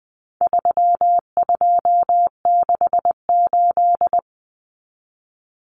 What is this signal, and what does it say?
Morse code, heard as a keyed tone, about 20 wpm, 700 Hz. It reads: 3268
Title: Numbers - 4-Digit 20wpm